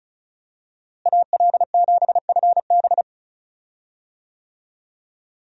Morse code, heard as a keyed tone, about 36 wpm, 700 Hz. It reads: AL7FB
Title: USA Call Signs - 5 Characters - Repeated 2x - 35wpm